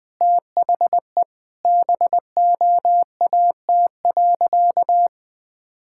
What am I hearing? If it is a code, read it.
Morse code, heard as a keyed tone, about 20 wpm, 700 Hz. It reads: THE BOAT.
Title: Sentences from Top 1000 Words - Review Part 5 - 20wpm